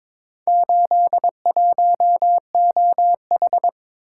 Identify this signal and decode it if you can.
Morse code, heard as a keyed tone, about 22 wpm, 700 Hz. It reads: 81OH
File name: practice / 22wpm / Four Letter-Number Combinations - 22wpm